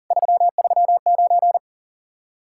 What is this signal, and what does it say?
Morse code, heard as a keyed tone, about 47 wpm, 700 Hz. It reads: 339
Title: Speed-Racing Common QSO Phrases - 40wpm